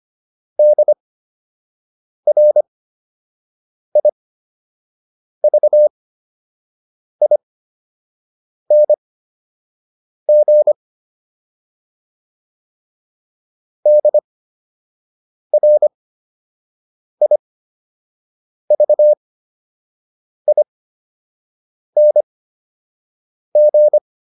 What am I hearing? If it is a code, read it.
Morse code, heard as a keyed tone, about 25 wpm, spaced as for 6 wpm, 600 Hz. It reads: DRIVING DRIVING